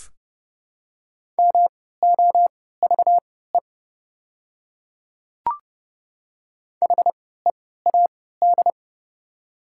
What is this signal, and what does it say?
Morse code, heard as a keyed tone, about 29 wpm, spaced as for 17 wpm, 690 Hz. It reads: MOVE HEAD
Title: 3x Character Spacing - Top 300 Words - 30wpm